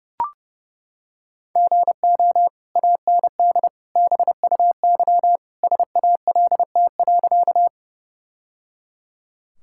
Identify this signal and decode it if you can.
Morse code, heard as a keyed tone, about 30 wpm, 700 Hz. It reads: GO AND BUY SALT.
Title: Sentences from Top 1000 Words - Review Part 5 - 30wpm